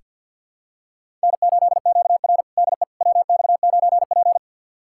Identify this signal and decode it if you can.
Morse code, heard as a keyed tone, about 48 wpm, 700 Hz. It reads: N8QR DE WX9P
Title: USA Callsign DE Callsign - 4 Characters - 50wpm